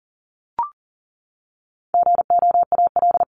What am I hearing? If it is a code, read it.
Morse code, heard as a keyed tone, about 40 wpm, 700 Hz. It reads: GOAL
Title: Top 751-1000 Words 40wpm